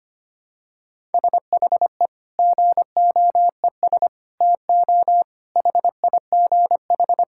Morse code, heard as a keyed tone, about 25 wpm, 700 Hz. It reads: SHE GOES TO HIGH
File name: practice / 25wpm / Sentences with 2-Word Phrases - 25wpm